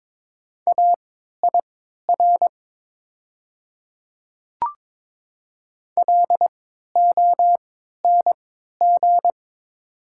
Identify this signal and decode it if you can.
Morse code, heard as a keyed tone, about 21 wpm, spaced as for 12 wpm, 700 Hz. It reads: AIR LONG
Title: 3x Character Spacing - Top 500 Words - 22wpm